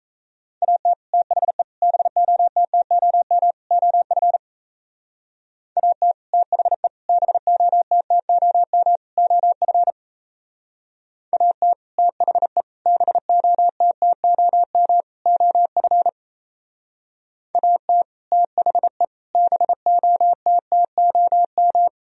Morse code, heard as a keyed tone, about 35 wpm, 700 Hz. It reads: AT THE BOTTOM OF AT THE BOTTOM OF AT THE BOTTOM OF AT THE BOTTOM
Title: Speed-Racing - Sets of 4 Words 28wpm